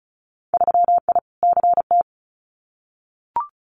Morse code, heard as a keyed tone, about 34 wpm, 690 Hz. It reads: 3I CT
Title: Winter Field Day Exchanges 35wpm